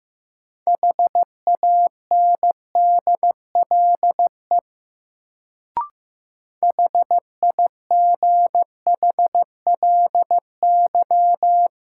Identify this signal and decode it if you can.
Morse code, heard as a keyed tone, about 15 wpm, 700 Hz. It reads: HANDLE HIGHLY
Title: Sentences from Top 1100 Words - New Word Drill - 15wpm